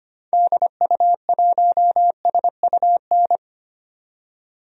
Morse code, heard as a keyed tone, about 25 wpm, 700 Hz. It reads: DU1SUN
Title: DXCC-100 Call Signs - 6 Characters - 25wpm